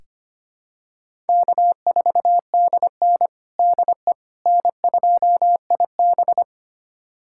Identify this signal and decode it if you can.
Morse code, heard as a keyed tone, about 25 wpm, 700 Hz. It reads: K4DN DE N2IB
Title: USA Callsign DE Callsign - 4 Characters - 25wpm